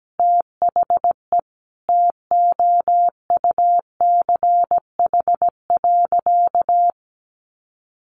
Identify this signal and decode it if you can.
Morse code, heard as a keyed tone, about 17 wpm, 700 Hz. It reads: THE TOUCH.
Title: Sentences from Top 1500 Words 17wpm